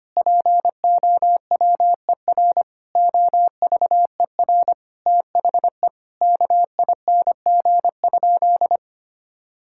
Morse code, heard as a keyed tone, about 25 wpm, 700 Hz. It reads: POWER OVER THE KING?
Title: Sentences from Top 1000 Words - Review Part 6 - 25wpm